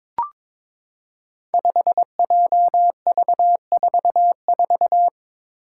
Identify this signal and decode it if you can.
Morse code, heard as a keyed tone, about 22 wpm, 700 Hz. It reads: 5JV44
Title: Five Challenging Letter-Number Combinations 22wpm